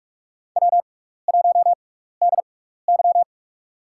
Morse code, heard as a keyed tone, about 46 wpm, spaced as for 16 wpm, 700 Hz. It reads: W1DY
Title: USA Call Signs - 4 Characters - 6x Character Spacing - 45wpm